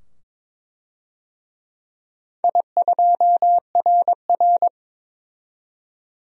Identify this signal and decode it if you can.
Morse code, heard as a keyed tone, about 23 wpm, 700 Hz. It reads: I2RR
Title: DXCC-100 Call Signs - 4 Characters - 22wpm